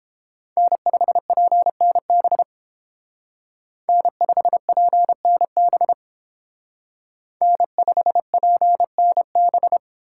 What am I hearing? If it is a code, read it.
Morse code, heard as a keyed tone, about 31 wpm, 700 Hz. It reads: N5PNB N5PNB N5PNB